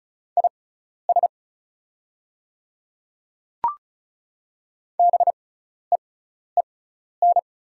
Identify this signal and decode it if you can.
Morse code, heard as a keyed tone, about 36 wpm, spaced as for 12 wpm, 700 Hz. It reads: IS BEEN
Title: 6x Character Spacing - Top 300 Words - 35wpm